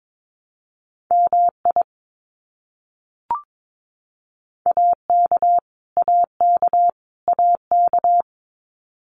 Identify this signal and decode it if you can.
Morse code, heard as a keyed tone, about 22 wpm, 700 Hz. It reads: MI AK AK AK
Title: US States Repeated 3x 22wpm